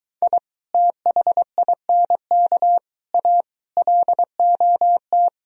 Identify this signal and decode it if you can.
Morse code, heard as a keyed tone, about 22 wpm, 700 Hz. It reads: I THINK A LOT